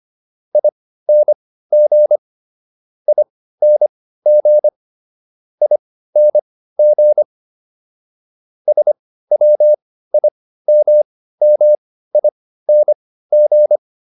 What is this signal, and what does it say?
Morse code, heard as a keyed tone, about 25 wpm, spaced as for 15 wpm, 600 Hz. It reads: ING ING ING SWIMMING